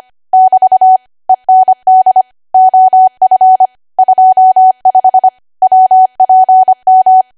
Morse code, heard as a keyed tone, about 25 wpm, 750 Hz. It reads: <BT> END OF 25 WPM
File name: Archive / 25 WPM / 25WPM